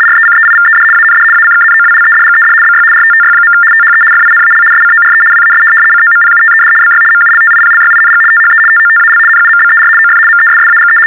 RTTY, about 75 baud. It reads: RECEIVED AND ACKNOWLEDGED BY RTTY. THE FINAL SCORE WAS COMPUTED BY MULTIPLYING THE TOTAL NUMBER OF MESSAGE
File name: teletype-RTTY-75